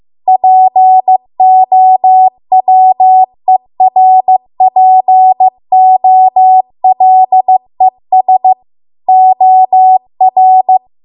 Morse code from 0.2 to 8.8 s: POWERPOLES